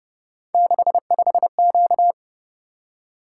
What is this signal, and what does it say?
Morse code, heard as a keyed tone, about 30 wpm, 700 Hz. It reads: 65Q